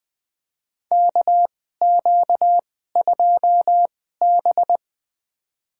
Morse code, heard as a keyed tone, about 20 wpm, spaced as for 14 wpm, 700 Hz. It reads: KQ2B
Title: USA Call Signs - 4 Characters - 2x Character Spacing - 20wpm